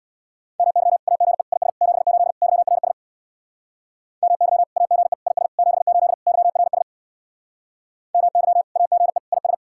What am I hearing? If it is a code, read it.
Morse code, heard as a keyed tone, about 66 wpm, 700 Hz. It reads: MY AGE IS 78 YRS MY AGE IS 78 YRS MY AGE IS